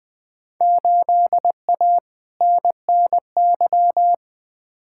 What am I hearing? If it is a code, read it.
Morse code, heard as a keyed tone, about 20 wpm, 700 Hz. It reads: 8A NNY
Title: Field Day Exchanges Encore - 20wpm